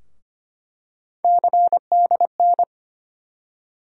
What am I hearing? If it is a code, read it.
Morse code, heard as a keyed tone, about 25 wpm, 700 Hz. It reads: CDN
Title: USA Call Sign Postfixes - 3 Letters 25wpm